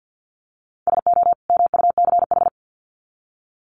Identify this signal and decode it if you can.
Morse code, heard as a keyed tone, about 50 wpm, 700 Hz. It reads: SO MUCH
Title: Sets of 2 Words 50wpm